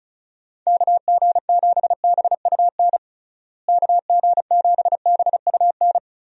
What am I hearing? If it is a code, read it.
Morse code, heard as a keyed tone, about 36 wpm, 700 Hz. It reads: KG7BUN KG7BUN
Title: USA Call Signs - 6 Characters - Repeated 2x - 35wpm